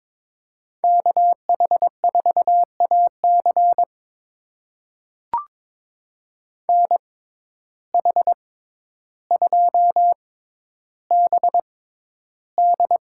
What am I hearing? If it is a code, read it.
Morse code, heard as a keyed tone, about 22 wpm, spaced as for 11 wpm, 700 Hz. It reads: KH4AC NH2BD
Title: USA Call Signs - 5 Characters - 6x Character Spacing - 22wpm